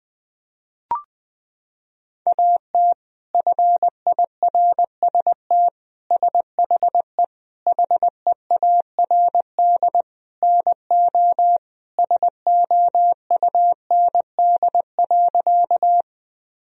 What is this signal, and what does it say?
Morse code, heard as a keyed tone, about 20 wpm, 700 Hz. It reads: AT FIRST SHE HEARD NO SOUND.
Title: Sentences from Top 1000 Words - Review Part 1 - 20wpm